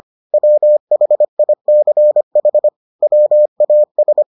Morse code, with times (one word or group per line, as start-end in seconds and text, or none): 0.3-2.7 WHICH
3.0-4.2 WAS